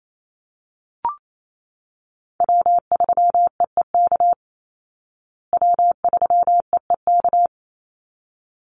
Morse code, heard as a keyed tone, about 28 wpm, 700 Hz. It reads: W3EEK W3EEK
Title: Call Signs Repeated 3x - 28wpm